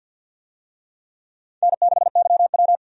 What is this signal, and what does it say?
Morse code, heard as a keyed tone, about 46 wpm, 700 Hz. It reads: N6YW
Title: USA Call Signs - 4 Characters - Repeated 2x - 50wpm